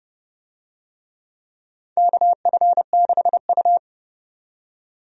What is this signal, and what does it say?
Morse code, heard as a keyed tone, about 32 wpm, 700 Hz. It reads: KF6U
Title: USA Call Signs - 4 Characters - 30wpm